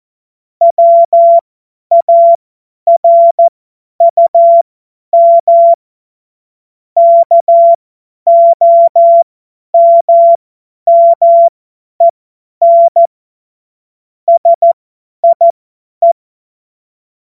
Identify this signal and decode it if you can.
Morse code, heard as a keyed tone, about 14 wpm, spaced as for 10 wpm, 680 Hz. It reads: WARUM KOMMEN SIE
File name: Morsekurs / Witze / Witz 16. Mai 2023 50 F70